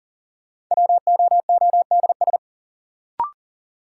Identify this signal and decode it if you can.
Morse code, heard as a keyed tone, about 40 wpm, 700 Hz. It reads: WOODS
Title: Sentences from Top 1000 Words - Review Part 2 - New Word Drill - 40wpm